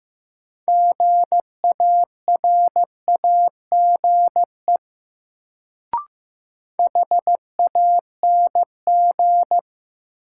Morse code, heard as a keyed tone, about 16 wpm, 700 Hz. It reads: GARAGE HANG
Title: Sentences from Top 1600 Words - New Word Drill Alt - 15wpm